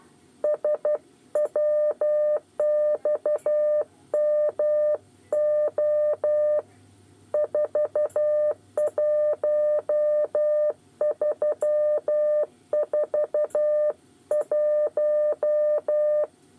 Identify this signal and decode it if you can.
Morse code, heard as a keyed tone, about 12 wpm, 600 Hz. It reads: SWXMO 41341
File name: 8 bis lezione cw 2